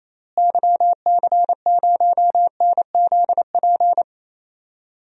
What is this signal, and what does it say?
Morse code, heard as a keyed tone, about 28 wpm, 700 Hz. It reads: YC0NZP
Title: DXCC-100 Call Signs - 6 Characters - Repeated 2x - 28wpm